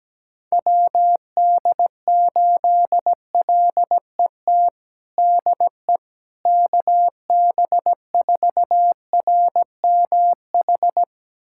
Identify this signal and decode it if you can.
Morse code, heard as a keyed tone, about 17 wpm, 700 Hz. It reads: WD8LET DE KB4RMH
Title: USA Callsign DE Callsign 17wpm